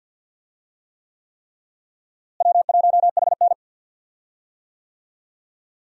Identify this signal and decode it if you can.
Morse code, heard as a keyed tone, about 52 wpm, 700 Hz. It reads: W1HN
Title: USA Call Signs - 4 Characters - Repeated 3x - 50wpm